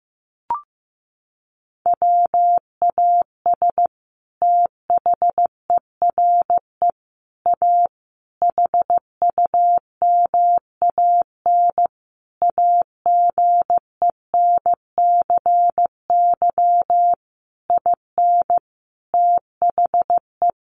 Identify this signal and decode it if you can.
Morse code, heard as a keyed tone, about 15 wpm, 700 Hz. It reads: WAS THERE A HUMAN AGENCY IN THE
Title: Sentences from Top 1300 Words 15wpm